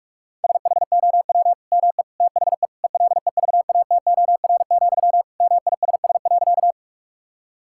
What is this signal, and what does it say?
Morse code, heard as a keyed tone, about 42 wpm, 700 Hz. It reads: SHOW ME THE ELEVATOR, MISS.
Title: Sentences from Top 1600 Words 45wpm